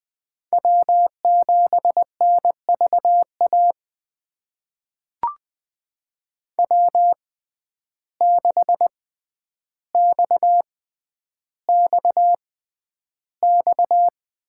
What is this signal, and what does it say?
Morse code, heard as a keyed tone, about 20 wpm, spaced as for 10 wpm, 700 Hz. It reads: W7NVA W6XXX
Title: USA Call Signs - 5 Characters - 6x Character Spacing - 20wpm